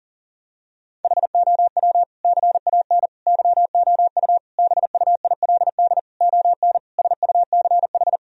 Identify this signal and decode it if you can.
Morse code, heard as a keyed tone, about 37 wpm, 700 Hz. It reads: HOW CAN YOU BUILD ON SUCH
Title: Sentences from Top 1000 Words - Review Part 1 - 40wpm